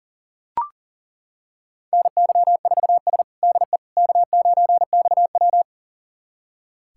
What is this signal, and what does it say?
Morse code, heard as a keyed tone, about 40 wpm, 700 Hz. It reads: NY4S DE K9XW
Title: USA Callsign DE Callsign - 4 Characters - 40wpm